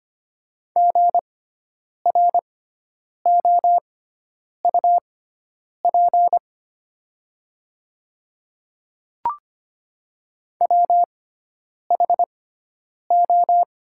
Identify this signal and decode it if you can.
Morse code, heard as a keyed tone, about 25 wpm, spaced as for 9 wpm, 700 Hz. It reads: GROUP WHO
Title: 6x Character Spacing - Top 500 Words - 25wpm